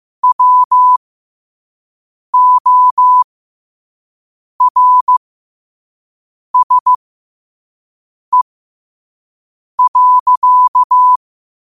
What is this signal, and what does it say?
Morse code, heard as a keyed tone, about 15 wpm, spaced as for 5 wpm, 1000 Hz. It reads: WORSE.
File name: qotdfiles / 5wpm